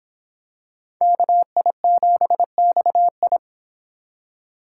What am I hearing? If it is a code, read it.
Morse code, heard as a keyed tone, about 23 wpm, 700 Hz. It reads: KI7XI